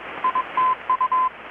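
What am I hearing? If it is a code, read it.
Morse code, heard as a keyed tone, about 21 wpm, 1020 Hz. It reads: ITU
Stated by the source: Morse from KPH's Night of Nights XX maritime radiotelegraphy event, received in Chiapas, Mexico about 3,800 km away on a Kenwood R-600 with a grounded 10 m long-wire antenna.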